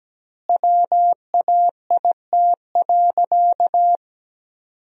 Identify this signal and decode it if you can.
Morse code, heard as a keyed tone, about 17 wpm, 700 Hz. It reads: WAIT.
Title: 2x Word-Spacing - Sentences from Top 500 Words - 17wpm